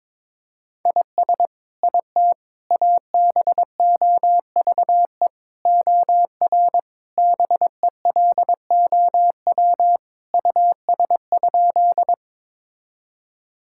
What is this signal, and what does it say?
Morse code, heard as a keyed tone, about 22 wpm, 700 Hz. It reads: IS IT ABOVE OR BELOW US?
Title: Sentences from Top 1000 Words - Review Part 5 - 22wpm